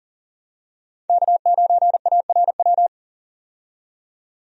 Morse code, heard as a keyed tone, about 40 wpm, 700 Hz. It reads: K9ARW
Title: USA Call Signs - 5 Characters - 40wpm